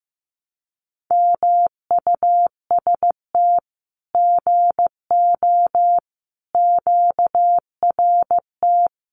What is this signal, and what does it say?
Morse code, heard as a keyed tone, about 15 wpm, 700 Hz. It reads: MUST GO QRT
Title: Common QSO Phrases 15wpm